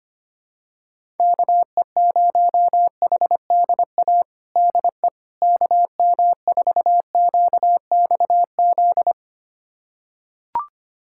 Morse code, heard as a keyed tone, about 25 wpm, 700 Hz. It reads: KE0HDA DE KM4QXZ
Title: USA Callsign DE Callsign - 6 Characters - 25wpm